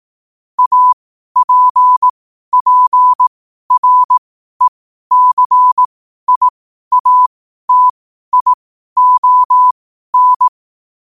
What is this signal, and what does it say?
Morse code, heard as a keyed tone, about 18 wpm, spaced as for 12 wpm, 1000 Hz. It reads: APPRECIATION
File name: qotdfiles / 12wpm